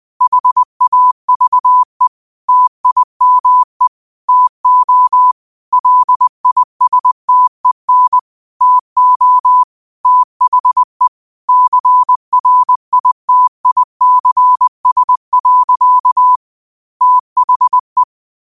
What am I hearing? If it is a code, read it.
Morse code, heard as a keyed tone, about 20 wpm, 1000 Hz. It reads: HAVE TIME TO LISTEN TO THE CRITICS. THE